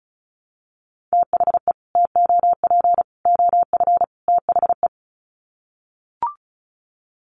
Morse code, heard as a keyed tone, about 34 wpm, 700 Hz. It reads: THE TOP OF THE
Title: Sets of 4 Words 35wpm